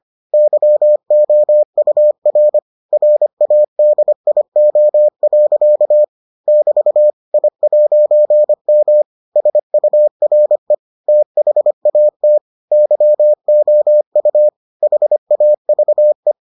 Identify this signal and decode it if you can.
Morse code, heard as a keyed tone, about 25 wpm, 600 Hz. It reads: YOUR RADIO. <BT> I'M SURE THAT YOU HAVE